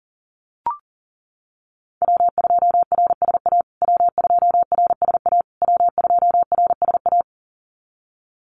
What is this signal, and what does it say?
Morse code, heard as a keyed tone, about 40 wpm, 700 Hz. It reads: W2RSA W2RSA W2RSA
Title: USA Call Signs Repeated 3x 40wpm